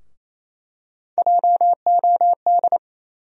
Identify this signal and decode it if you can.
Morse code, heard as a keyed tone, about 28 wpm, 700 Hz. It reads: JOD
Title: USA Call Sign Postfixes - 3 Letters 28wpm